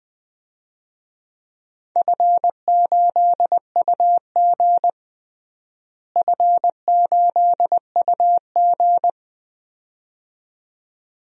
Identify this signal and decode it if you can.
Morse code, heard as a keyed tone, about 20 wpm, 700 Hz. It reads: F8UG F8UG
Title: DXCC-100 Call Signs - 4 Characters - Repeated 2x - 20wpm